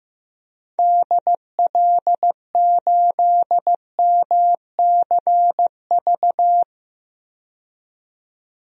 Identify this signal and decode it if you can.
Morse code, heard as a keyed tone, about 16 wpm, 700 Hz. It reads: DL8MCV
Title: DXCC-100 Call Signs - 6 Characters - 15wpm